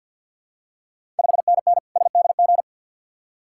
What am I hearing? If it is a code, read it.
Morse code, heard as a keyed tone, about 50 wpm, 690 Hz. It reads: 5NN SDG
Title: Ontario QSO Party Exchange 50wpm